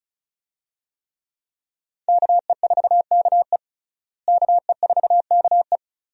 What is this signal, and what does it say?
Morse code, heard as a keyed tone, about 36 wpm, 700 Hz. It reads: KE4KE KE4KE
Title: USA Call Signs - 5 Characters - Repeated 2x - 35wpm